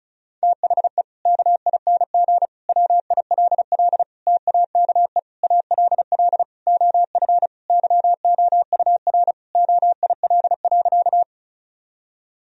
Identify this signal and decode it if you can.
Morse code, heard as a keyed tone, about 34 wpm, 700 Hz. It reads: THE KING WILL TAKE ALL OF YOUR OIL.
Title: Sentences from Top 1000 Words - Review Part 4 - 35wpm